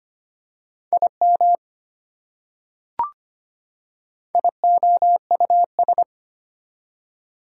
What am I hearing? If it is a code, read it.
Morse code, heard as a keyed tone, about 25 wpm, 700 Hz. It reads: IM IOUS
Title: Prefixes and Suffixes 25wpm